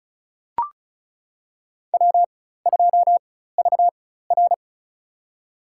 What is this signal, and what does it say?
Morse code, heard as a keyed tone, about 36 wpm, spaced as for 16 wpm, 700 Hz. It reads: W2VR
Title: USA Call Signs - 4 Characters - 4x Character Spacing - 35wpm